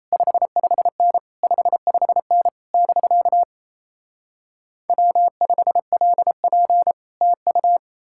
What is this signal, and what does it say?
Morse code, heard as a keyed tone, about 38 wpm, 700 Hz. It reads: 55N 55N <BK> W5LP TU